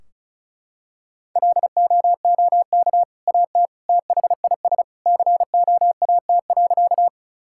Morse code, heard as a keyed tone, about 34 wpm, 700 Hz. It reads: LOOK AT THIS COAT.
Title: Sentences from Top 1600 Words 35wpm